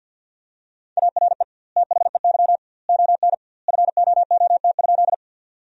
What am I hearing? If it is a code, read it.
Morse code, heard as a keyed tone, about 48 wpm, 700 Hz. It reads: ARE THEY ON FOOT?
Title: 2x Word-Spacing - Sentences from Top 500 Words - 50wpm